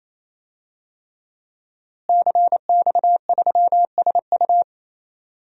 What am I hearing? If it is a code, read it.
Morse code, heard as a keyed tone, about 29 wpm, 700 Hz. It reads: CX3SU
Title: DXCC-100 Call Signs - 5 Characters - Repeated 3x - 28wpm